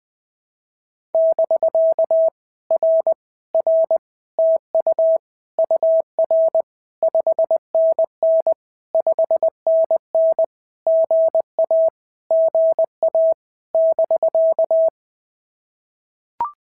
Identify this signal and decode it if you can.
Morse code, heard as a keyed tone, about 20 wpm, 650 Hz. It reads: <BK> R R TU UR 5NN 5NN GA GA <BK>
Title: POTA QSO Practice - 20wpm